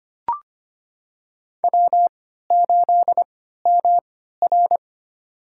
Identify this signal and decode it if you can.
Morse code, heard as a keyed tone, about 25 wpm, spaced as for 14 wpm, 700 Hz. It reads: W8MR
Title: USA Call Signs - 3x Character Spacing - 25wpm